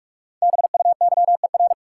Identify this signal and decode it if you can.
Morse code, heard as a keyed tone, about 47 wpm, 700 Hz. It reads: BUYER
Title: Five Letter Words - 45wpm